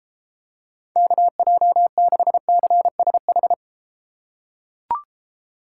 Morse code, heard as a keyed tone, about 34 wpm, 700 Hz. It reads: KJ6CSH